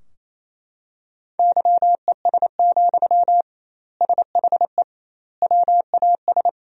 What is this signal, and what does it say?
Morse code, heard as a keyed tone, about 28 wpm, 700 Hz. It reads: YES, SHE WAS
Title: 2x Word-Spacing - Sentences from Top 500 Words - 28wpm